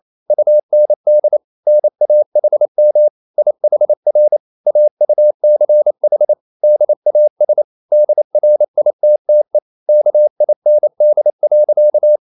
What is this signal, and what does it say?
Morse code, heard as a keyed tone, about 28 wpm, 600 Hz. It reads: UND NAHM IHR AUCH DAS DRITTE KIND.
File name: marienkind9 28wpm